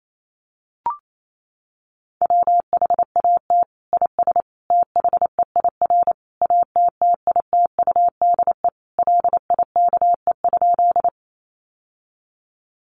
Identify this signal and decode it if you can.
Morse code, heard as a keyed tone, about 28 wpm, 700 Hz. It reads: WHAT IS THEIR ATTITUDE LIKE?
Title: Sentences from Top 1300 Words 28wpm